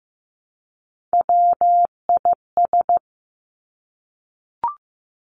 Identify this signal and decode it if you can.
Morse code, heard as a keyed tone, about 15 wpm, 700 Hz. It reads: WIS
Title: Three Challenging Letter-Number Combinations 15wpm